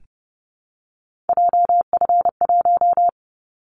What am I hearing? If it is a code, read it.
Morse code, heard as a keyed tone, about 32 wpm, 700 Hz. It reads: JF1
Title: Callsign Prefixes for Top 25 DXCC - 30wpm